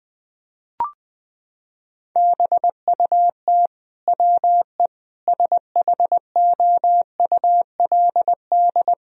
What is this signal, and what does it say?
Morse code, heard as a keyed tone, about 20 wpm, 700 Hz. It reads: BUT WE SHOULD
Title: Sentences from Top 1000 Words - Review Part 4 - 20wpm